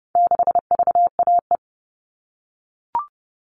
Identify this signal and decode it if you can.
Morse code, heard as a keyed tone, about 30 wpm, 700 Hz. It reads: 6VAE
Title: All Cut Numbers - 4-Digit 30wpm